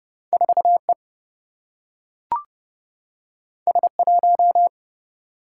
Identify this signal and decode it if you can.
Morse code, heard as a keyed tone, about 30 wpm, 700 Hz. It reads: 4E S1
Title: Single-Letter Single-Number 30wpm